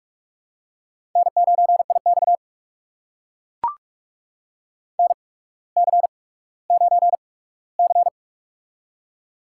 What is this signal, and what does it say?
Morse code, heard as a keyed tone, about 44 wpm, spaced as for 19 wpm, 700 Hz. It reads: N9IX NC9C
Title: USA Call Signs - 4 Characters - 8x Character Spacing - 45wpm